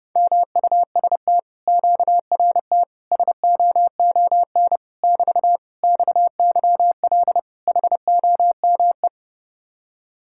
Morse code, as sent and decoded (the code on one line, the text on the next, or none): -- ..- ... - / --.- .-. - / ... --- --- -. / -...- / -..- -.-- .-.. / .... --- -- .
MUST QRT SOON <BT> XYL HOME